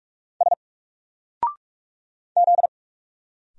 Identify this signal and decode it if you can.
Morse code, heard as a keyed tone, about 45 wpm, 700 Hz. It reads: S Z
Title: Single Letter-Number 45wpm